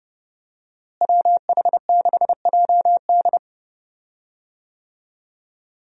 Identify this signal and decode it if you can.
Morse code, heard as a keyed tone, about 30 wpm, 700 Hz. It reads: WH6JD
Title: USA Call Signs - 5 Characters - Repeated 3x - 30wpm